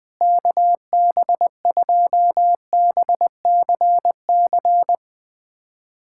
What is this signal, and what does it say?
Morse code, heard as a keyed tone, about 20 wpm, 700 Hz. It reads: KB2BCC
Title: USA Callsign DE Callsign - 6 Characters - 20wpm